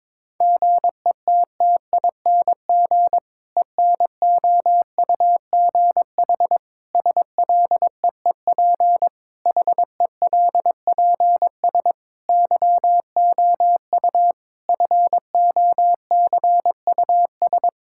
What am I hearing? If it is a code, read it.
Morse code, heard as a keyed tone, about 22 wpm, 700 Hz. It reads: GETTING ENOUGH SLEEP HELPS YOU FOCUS
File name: Difficult Sentences with Hints - 22wpm